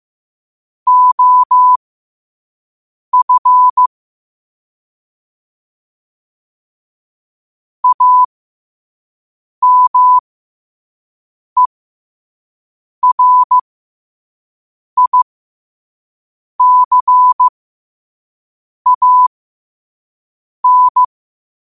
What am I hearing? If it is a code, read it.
Morse code, heard as a keyed tone, about 15 wpm, spaced as for 5 wpm, 1000 Hz. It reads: OF AMERICAN
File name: qotdfiles / 5wpm